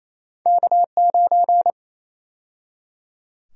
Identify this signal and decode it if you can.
Morse code, heard as a keyed tone, about 28 wpm, 700 Hz. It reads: K9
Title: USA Call Sign Prefixes - 28wpm